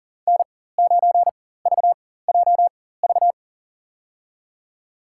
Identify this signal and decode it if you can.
Morse code, heard as a keyed tone, about 40 wpm, spaced as for 19 wpm, 700 Hz. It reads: N9VJV
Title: USA Call Signs - 5 Characters - 4x Character Spacing - 40wpm